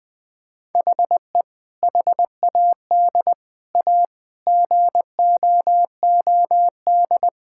Morse code, heard as a keyed tone, about 20 wpm, 700 Hz. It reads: HE HAD A GOOD
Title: Sentences from Top 1000 Words - Review Part 6 - 20wpm